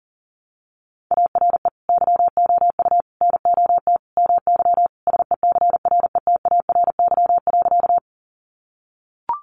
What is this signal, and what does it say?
Morse code, heard as a keyed tone, about 40 wpm, 700 Hz. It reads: ARE YOU NOT MY SECRETARY.
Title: Sentences from Top 1300 Words 40wpm